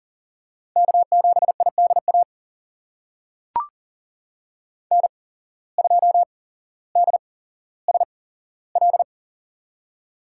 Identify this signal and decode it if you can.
Morse code, heard as a keyed tone, about 38 wpm, spaced as for 17 wpm, 700 Hz. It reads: K7IDA N2DSL
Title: USA Call Signs - 5 Characters - 8x Character Spacing - 40wpm